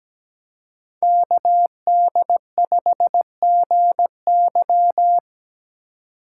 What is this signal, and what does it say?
Morse code, heard as a keyed tone, about 20 wpm, 700 Hz. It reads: KD5GY
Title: Speed Racing - USA 2x2 Call Signs 17wpm